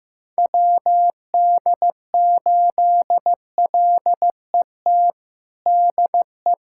Morse code, heard as a keyed tone, about 15 wpm, 700 Hz. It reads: WD8LET DE
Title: USA Callsign DE Callsign 15wpm